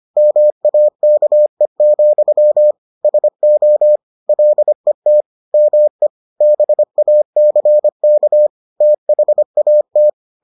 Morse code, heard as a keyed tone, about 25 wpm, 600 Hz. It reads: MAKE, SO LET ME BACK THAT